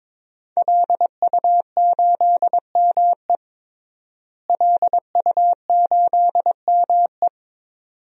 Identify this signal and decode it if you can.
Morse code, heard as a keyed tone, about 22 wpm, 700 Hz. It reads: LU8ME LU8ME
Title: DXCC-100 Call Signs - 5 Characters - Repeated 2x - 22wpm